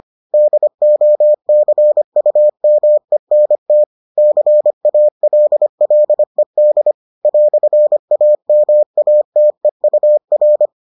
Morse code, heard as a keyed tone, about 25 wpm, 600 Hz. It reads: DOCUMENT CALLED "AMATEUR